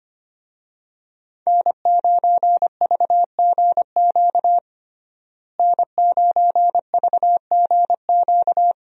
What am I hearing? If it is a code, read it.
Morse code, heard as a keyed tone, about 25 wpm, 700 Hz. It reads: N9VGQ N9VGQ
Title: USA Call Signs - 5 Characters - Repeated 2x - 25wpm